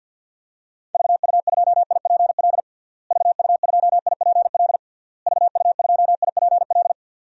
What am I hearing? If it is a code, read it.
Morse code, heard as a keyed tone, about 51 wpm, 700 Hz. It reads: VU2IPL VU2IPL VU2IPL
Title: DXCC-100 Call Signs - 6 Characters - Repeated 3x - 50wpm